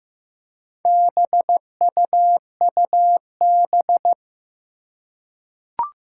Morse code, heard as a keyed tone, about 15 wpm, 700 Hz. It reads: BUUB
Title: Four Challenging Letter-Number Combinations 15wpm